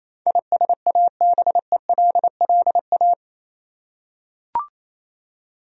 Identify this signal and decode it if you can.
Morse code, heard as a keyed tone, about 28 wpm, 700 Hz. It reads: ISABELLA
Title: USA Common Names - 28wpm